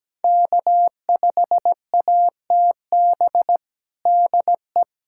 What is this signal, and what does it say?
Morse code, heard as a keyed tone, about 17 wpm, 700 Hz. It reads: K5ATB DE
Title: USA Callsign DE Callsign 17wpm